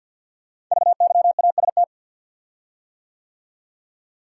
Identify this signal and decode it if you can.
Morse code, heard as a keyed tone, about 52 wpm, 700 Hz. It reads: VYAST
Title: Code Groups - 5 Letters - 50wpm